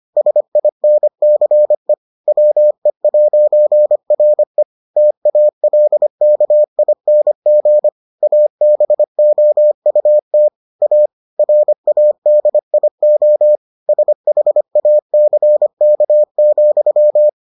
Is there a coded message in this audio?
Yes: Morse code from 0.2 to 17.3 s, heard as a keyed tone, about 25 wpm, 600 Hz. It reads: SINCE WE'RE TALKING ABOUT A RADIO SHACK,